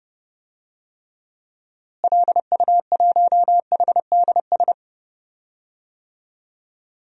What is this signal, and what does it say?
Morse code, heard as a keyed tone, about 31 wpm, 700 Hz. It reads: LU1HDS
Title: DXCC-100 Call Signs - 6 Characters - Repeated 3x - 30wpm